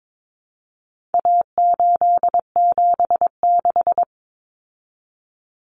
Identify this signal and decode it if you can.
Morse code, heard as a keyed tone, about 22 wpm, 700 Hz. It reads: A876
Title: Common Cut Numbers - 4-Digit 22wpm